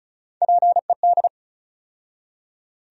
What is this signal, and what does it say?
Morse code, heard as a keyed tone, about 34 wpm, 700 Hz. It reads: PED
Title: USA Call Sign Postfixes - 3 Letters 35wpm